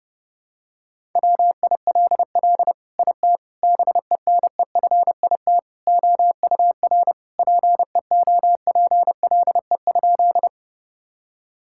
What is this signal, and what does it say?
Morse code, heard as a keyed tone, about 32 wpm, 700 Hz. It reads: WILL IT BENEFIT OUR PEOPLE?
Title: Sentences from Top 1000 Words - Review Part 6 - 30wpm